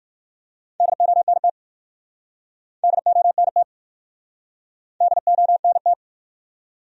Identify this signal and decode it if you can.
Morse code, heard as a keyed tone, about 55 wpm, 700 Hz. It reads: DONT DONT DONT
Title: Speed-Racing Top 300 Words 45wpm